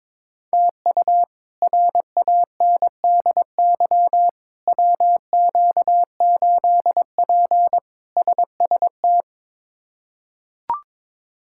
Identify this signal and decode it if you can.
Morse code, heard as a keyed tone, about 22 wpm, 700 Hz. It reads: TU RANDY WQ8P SST
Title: K1USN SST QSO Practice - 22wpm